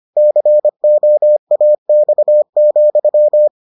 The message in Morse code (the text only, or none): COAX,